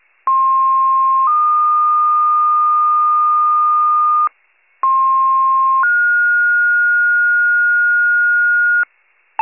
Two-tone page, heard as a one-second tone, then a three-second tone, about 1060 then 1220 Hz, about 1060 then 1430 Hz, with noise. It is heard from 0.3 to 8.8 s.
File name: pager / 98 FIRE / wcalert